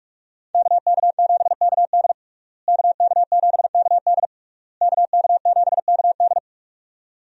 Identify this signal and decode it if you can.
Morse code, heard as a keyed tone, about 46 wpm, 700 Hz. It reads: KK7KD KK7KD KK7KD
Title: USA Call Signs - 5 Characters - Repeated 3x - 45wpm